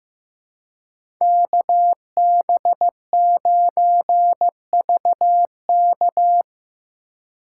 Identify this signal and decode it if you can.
Morse code, heard as a keyed tone, about 15 wpm, 700 Hz. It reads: KB9VK
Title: USA Call Signs - 5 Characters - 4x Character Spacing - 15wpm